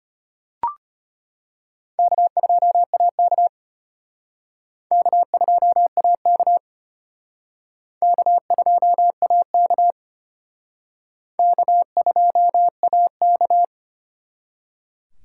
Morse code, heard as a keyed tone, about 32 wpm, 700 Hz. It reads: K2AK K2AK K2AK K2AK